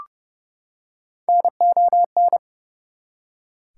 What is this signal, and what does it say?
Morse code, heard as a keyed tone, about 30 wpm, 700 Hz. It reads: NON
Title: Prefixes and Suffixes 30wpm